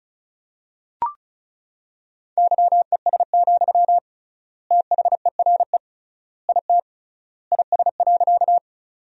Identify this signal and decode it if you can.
Morse code, heard as a keyed tone, about 36 wpm, 700 Hz. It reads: YES, THERE IT IS.
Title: 3x Word-Spacing - Sentences from Top 500 Words - 35wpm